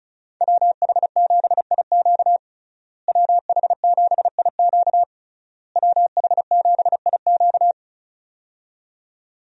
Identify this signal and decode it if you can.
Morse code, heard as a keyed tone, about 36 wpm, 700 Hz. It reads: WH7IQ WH7IQ WH7IQ
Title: USA Call Signs - 5 Characters - Repeated 3x - 35wpm